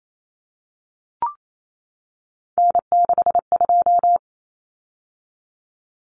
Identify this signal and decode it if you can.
Morse code, heard as a keyed tone, about 28 wpm, 700 Hz. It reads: N62
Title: Common Cut Numbers - 3-Digit 28wpm